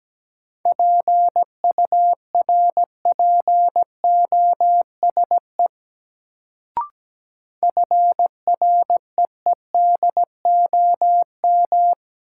Morse code, heard as a keyed tone, about 17 wpm, 700 Hz. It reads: PURPOSE FREEDOM
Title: Sentences from Top 800 Words - New Word Drill - 17wpm